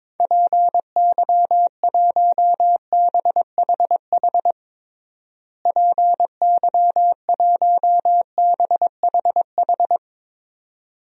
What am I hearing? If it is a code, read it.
Morse code, heard as a keyed tone, about 22 wpm, 700 Hz. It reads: PY1BHH PY1BHH
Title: DXCC-100 Call Signs - 6 Characters - Repeated 2x - 22wpm